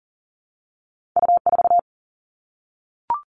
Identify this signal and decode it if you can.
Morse code, heard as a keyed tone, about 42 wpm, 700 Hz. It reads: U4
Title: All Cut Numbers - 2-Digit 40wpm